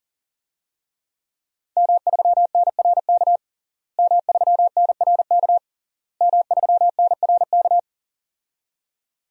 Morse code, heard as a keyed tone, about 40 wpm, 700 Hz. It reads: M3NRK M3NRK M3NRK
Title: DXCC-100 Call Signs - 5 Characters - Repeated 3x - 40wpm